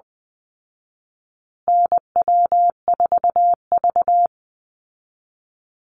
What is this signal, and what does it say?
Morse code, heard as a keyed tone, about 20 wpm, 700 Hz. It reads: NW4V
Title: Call Signs Repeated 3x - 20wpm